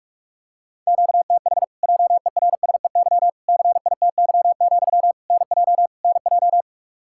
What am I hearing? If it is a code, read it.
Morse code, heard as a keyed tone, about 42 wpm, 700 Hz. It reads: QTH JERSEY CITY, NJ NJ
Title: Practice QSO - Exchange 1 - Beginning - 45wpm